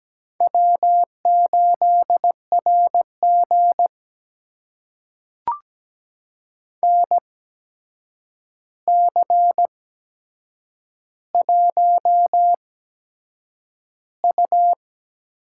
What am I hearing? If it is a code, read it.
Morse code, heard as a keyed tone, about 17 wpm, spaced as for 7 wpm, 700 Hz. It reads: W8RG NC1U
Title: USA Call Signs - 4 Characters - 8x Character Spacing - 17wpm